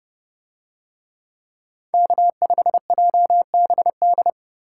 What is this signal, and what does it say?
Morse code, heard as a keyed tone, about 30 wpm, 700 Hz. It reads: K5JBD